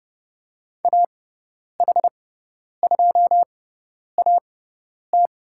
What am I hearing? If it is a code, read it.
Morse code, heard as a keyed tone, about 30 wpm, spaced as for 10 wpm, 700 Hz. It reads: AH2AT
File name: Lesson 17g - Letter u - Call Signs 30wpm